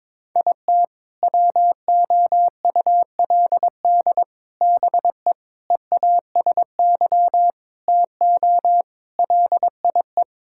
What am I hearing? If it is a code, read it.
Morse code, heard as a keyed tone, about 22 wpm, 700 Hz. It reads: IT WOULD BE EASY TO LIE
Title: Sentences from Top 1600 Words 22wpm